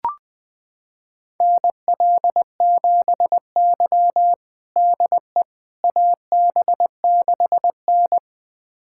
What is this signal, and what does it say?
Morse code, heard as a keyed tone, about 20 wpm, 700 Hz. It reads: NL7Y DE AB6N
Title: USA Callsign DE Callsign - 4 Characters - 20wpm